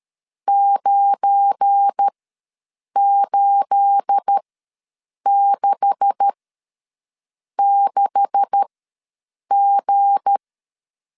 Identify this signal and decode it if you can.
Morse code, heard as a keyed tone, about 13 wpm, spaced as for 7 wpm, 800 Hz. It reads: 9866G
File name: LEZIONI / 9 286 ws 12wpm 3351